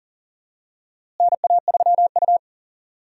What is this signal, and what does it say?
Morse code, heard as a keyed tone, about 40 wpm, 700 Hz. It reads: NA3U
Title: USA Call Signs - 4 Characters - Repeated 3x - 40wpm